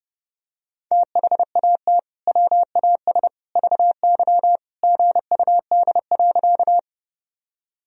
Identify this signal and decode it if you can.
Morse code, heard as a keyed tone, about 30 wpm, 700 Hz. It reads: THAT WAS VY GUD.
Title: Sentences from Top 200 Words and Abbreviations 30wpm